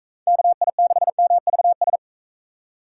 Morse code, heard as a keyed tone, about 44 wpm, 700 Hz. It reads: KI6MVS